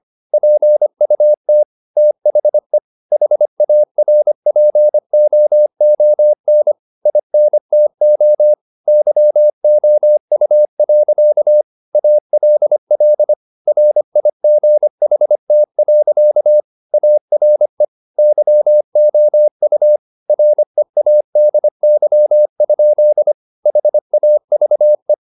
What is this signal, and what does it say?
Morse code, heard as a keyed tone, about 25 wpm, 600 Hz. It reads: PUT THE HARPOON INTO YOU. ALL RIGHT. ARE YOU READY? HAVE
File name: the old man and the sea part 1 25wpm